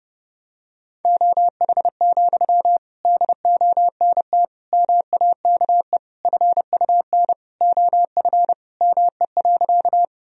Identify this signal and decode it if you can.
Morse code, heard as a keyed tone, about 30 wpm, 700 Hz. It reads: OH, DONT MAKE FUN OF ME.
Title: Sentences from Top 1000 Words - Review Part 4 - 30wpm